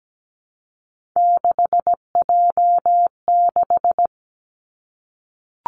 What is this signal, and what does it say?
Morse code, heard as a keyed tone, about 17 wpm, 700 Hz. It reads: 6J6
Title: Three Challenging Letter-Number Combinations 17wpm